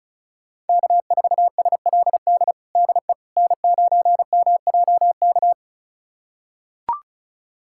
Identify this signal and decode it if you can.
Morse code, heard as a keyed tone, about 36 wpm, 700 Hz. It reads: K4SLD DE N9MJK
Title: USA Callsign DE Callsign - 5 Characters - 35wpm